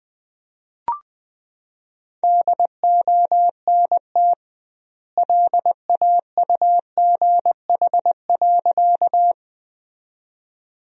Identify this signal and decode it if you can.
Morse code, heard as a keyed tone, about 20 wpm, 700 Hz. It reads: DONT LAUGH.
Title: 2x Word-Spacing - Sentences from Top 500 Words - 20wpm